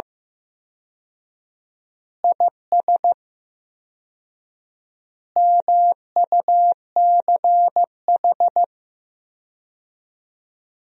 Morse code, heard as a keyed tone, about 15 wpm, 700 Hz. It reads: IS MUCH
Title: 4x Word-Spacing - Sentences from Top 400 Words - 15wpm